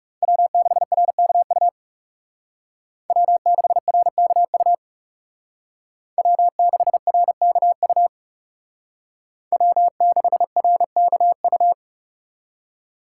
Morse code, heard as a keyed tone, about 36 wpm, 700 Hz. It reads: W6RKU W6RKU W6RKU W6RKU
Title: Speed Racing - USA 1x3 Call Signs 30wpm